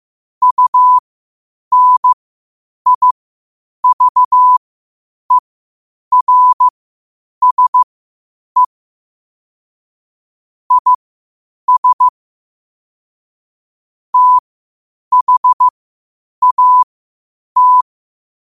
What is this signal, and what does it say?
Morse code, heard as a keyed tone, about 15 wpm, spaced as for 8 wpm, 1000 Hz. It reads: UNIVERSE IS THAT
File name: qotdfiles / 8wpm